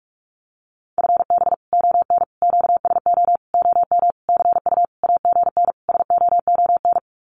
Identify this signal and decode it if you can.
Morse code, heard as a keyed tone, about 44 wpm, 700 Hz. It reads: FB ON QSO OM CU AGN SOON
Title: Common QSO Phrases 45wpm